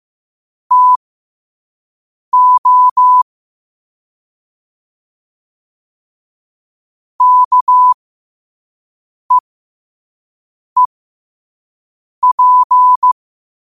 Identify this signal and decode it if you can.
Morse code, heard as a keyed tone, about 15 wpm, spaced as for 5 wpm, 1000 Hz. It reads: TO KEEP
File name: qotdfiles / 5wpm